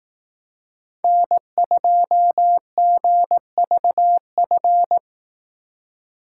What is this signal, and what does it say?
Morse code, heard as a keyed tone, about 18 wpm, 700 Hz. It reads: N2GVF
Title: Speed Racing - USA 1x3 Call Signs 15wpm